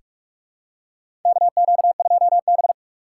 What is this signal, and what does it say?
Morse code, heard as a keyed tone, about 44 wpm, 700 Hz. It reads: KQ2B
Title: USA Call Signs - 4 Characters - 6x Character Spacing - 45wpm